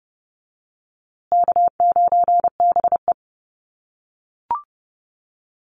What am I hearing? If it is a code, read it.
Morse code, heard as a keyed tone, about 30 wpm, 700 Hz. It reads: K9BE
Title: USA Call Signs Repeated 2x 30wpm